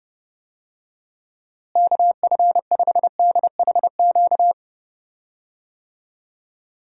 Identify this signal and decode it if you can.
Morse code, heard as a keyed tone, about 30 wpm, 700 Hz. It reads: KF5DHQ
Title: USA Call Signs - 6 Characters - Repeated 3x - 30wpm